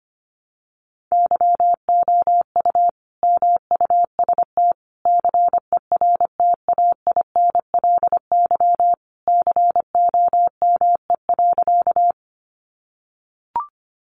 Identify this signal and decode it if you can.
Morse code, heard as a keyed tone, about 25 wpm, 700 Hz. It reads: YOU MUST CERTAINLY COME.
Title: Sentences from Top 900 Words 25wpm